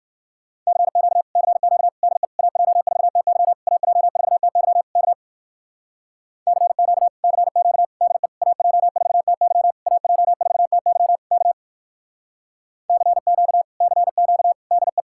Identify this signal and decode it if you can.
Morse code, heard as a keyed tone, about 54 wpm, 700 Hz. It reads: CQ CQ DE AJ4TY AJ4TY K CQ CQ DE AJ4TY AJ4TY K CQ CQ DE
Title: Speed-Racing Common QSO Phrases Encore - 45wpm